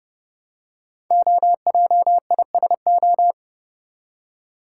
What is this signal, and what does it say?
Morse code, heard as a keyed tone, about 30 wpm, 700 Hz. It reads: OJISO
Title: Code Groups - 5 Letters - 30wpm